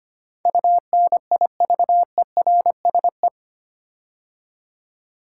Morse code, heard as a keyed tone, about 25 wpm, 690 Hz. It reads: UNIVERSE
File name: 8 Letter Words - First 3 Letters Given - 25wpm